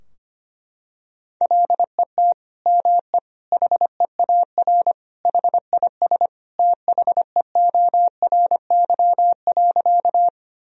Morse code, heard as a keyed tone, about 25 wpm, 700 Hz. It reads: LET ME HEAR HIS THEORY.
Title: Sentences from Top 1000 Words - Review Part 3 - 25wpm